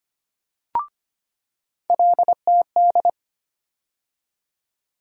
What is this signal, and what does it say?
Morse code, heard as a keyed tone, about 25 wpm, 700 Hz. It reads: LTD
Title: USA Call Sign Postfixes - 3 Letters 25wpm